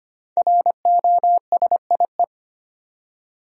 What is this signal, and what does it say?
Morse code, heard as a keyed tone, about 25 wpm, 700 Hz. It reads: ROSIE
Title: USA Common Names - 25wpm